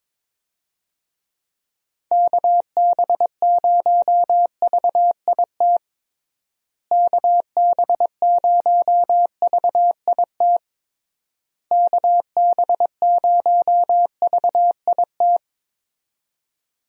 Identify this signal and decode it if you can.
Morse code, heard as a keyed tone, about 22 wpm, 700 Hz. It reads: KB0VIT KB0VIT KB0VIT
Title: USA Call Signs - 6 Characters - Repeated 3x - 22wpm